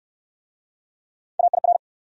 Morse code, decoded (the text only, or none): AIR